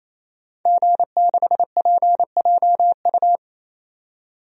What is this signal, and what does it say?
Morse code, heard as a keyed tone, about 27 wpm, 700 Hz. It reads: G6PJU
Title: DXCC-100 Call Signs - 5 Characters - 28wpm